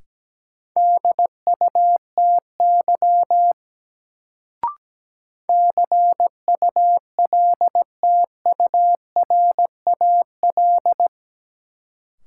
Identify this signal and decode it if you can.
Morse code, heard as a keyed tone, about 17 wpm, 700 Hz. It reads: DUTY CULTURAL
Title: Sentences from Top 1500 Words - New Word Drill Alt - 17wpm